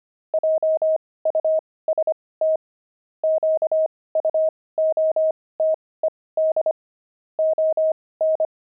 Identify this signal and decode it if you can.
Morse code, heard as a keyed tone, about 25 wpm, spaced as for 18 wpm, 630 Hz. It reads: JUST QUOTED ON
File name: pangram-fs-18-cs-25-nine